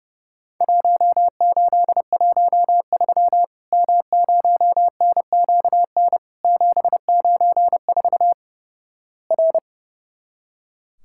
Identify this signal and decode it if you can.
Morse code, heard as a keyed tone, about 32 wpm, 700 Hz. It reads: 1813 M0NQN 794 R
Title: Worked All Europe DX Contest - QTC Exchange - 30wpm